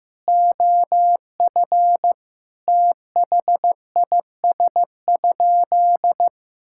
Morse code, heard as a keyed tone, about 15 wpm, 700 Hz. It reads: OF THIS?
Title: Sentences from Top 1000 Words - Review Part 6 - 15wpm